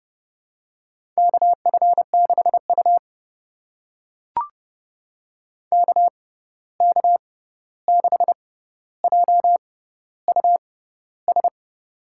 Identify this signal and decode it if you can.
Morse code, heard as a keyed tone, about 32 wpm, spaced as for 14 wpm, 700 Hz. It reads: KF6U KK6JUS
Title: USA Call Signs - 6x Character Spacing - 30wpm